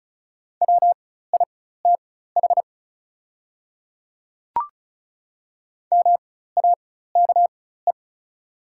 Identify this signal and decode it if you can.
Morse code, heard as a keyed tone, about 36 wpm, spaced as for 16 wpm, 700 Hz. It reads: WITH MAKE
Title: 4x Character Spacing - Top 100 Words - 35wpm